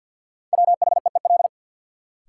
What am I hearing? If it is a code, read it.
Morse code, heard as a keyed tone, about 47 wpm, 700 Hz. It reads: WHEEL
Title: Sentences from Top 500 Words - New Word Drill - 50wpm